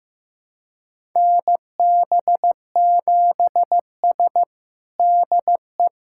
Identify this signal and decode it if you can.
Morse code, heard as a keyed tone, about 15 wpm, 700 Hz. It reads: NB7S DE
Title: USA Callsign DE Callsign - 4 Characters - 15wpm